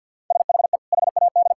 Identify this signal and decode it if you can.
Morse code, heard as a keyed tone, about 50 wpm, 700 Hz. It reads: SHE HAD
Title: Sentences from Top 1000 Words - Review Part 5 - 50wpm